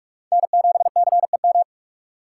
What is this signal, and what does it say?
Morse code, heard as a keyed tone, about 46 wpm, 700 Hz. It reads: N7CEM
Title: USA Call Signs - 5 Characters - 3x Character Spacing - 45wpm